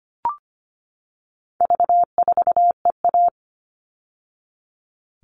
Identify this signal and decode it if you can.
Morse code, heard as a keyed tone, about 25 wpm, 700 Hz. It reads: V4EA
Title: All Cut Numbers - 4-Digit 25wpm